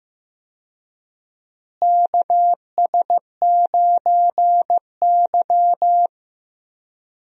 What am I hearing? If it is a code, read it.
Morse code, heard as a keyed tone, about 15 wpm, 700 Hz. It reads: KS9Y
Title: USA Call Signs - 4 Characters - Repeated 3x - 15wpm